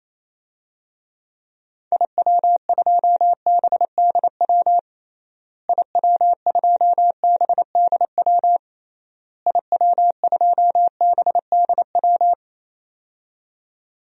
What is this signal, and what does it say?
Morse code, heard as a keyed tone, about 28 wpm, 700 Hz. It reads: IW2BDW IW2BDW IW2BDW
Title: DXCC-100 Call Signs - 6 Characters - Repeated 3x - 28wpm